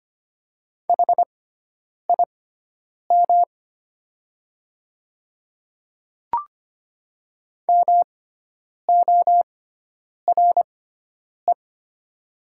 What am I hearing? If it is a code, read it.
Morse code, heard as a keyed tone, about 25 wpm, spaced as for 9 wpm, 700 Hz. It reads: HIM MORE
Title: 6x Character Spacing - Top 100 Words - 25wpm